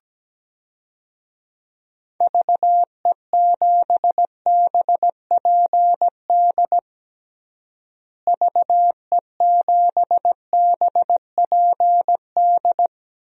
Morse code, heard as a keyed tone, about 17 wpm, 700 Hz. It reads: VE7BPD VE7BPD
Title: DXCC-100 Call Signs - 6 Characters - Repeated 3x - 17wpm